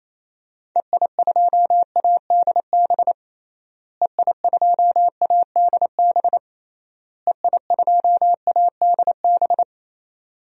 Morse code, heard as a keyed tone, about 28 wpm, 700 Hz. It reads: EI2ADB EI2ADB EI2ADB
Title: DXCC-100 Call Signs - 6 Characters - Repeated 3x - 28wpm